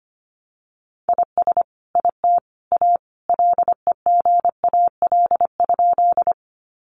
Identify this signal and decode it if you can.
Morse code, heard as a keyed tone, about 25 wpm, 700 Hz. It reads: IS IT A LEGAL?
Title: Sentences from Top 1200 Words 25wpm